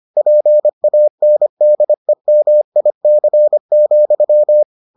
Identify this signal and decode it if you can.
Morse code, heard as a keyed tone, about 25 wpm, 600 Hz. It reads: PANDEMIC,